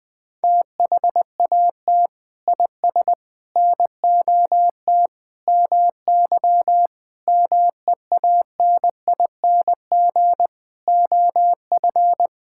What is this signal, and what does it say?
Morse code, heard as a keyed tone, about 21 wpm, 700 Hz. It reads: THAT IS NOT MY MEANING OF
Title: Sentences from Top 1000 Words - Review Part 6 - 20wpm